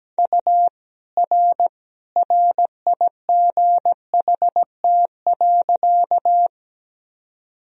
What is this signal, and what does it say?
Morse code, heard as a keyed tone, about 17 wpm, 700 Hz. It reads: U R RIGHT.
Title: Sentences from Top 200 Words and Abbreviations 17wpm